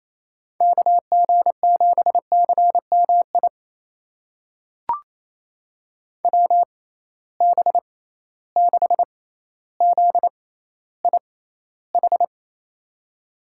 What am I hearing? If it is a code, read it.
Morse code, heard as a keyed tone, about 27 wpm, spaced as for 14 wpm, 700 Hz. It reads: KG7CMI WB6ZIH
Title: USA Call Signs - 6 Characters - 6x Character Spacing - 28wpm